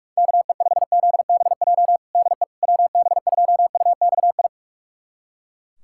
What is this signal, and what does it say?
Morse code, heard as a keyed tone, about 46 wpm, 700 Hz. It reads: KE5ZBJ DE WB2UXI
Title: USA Callsign DE Callsign 45wpm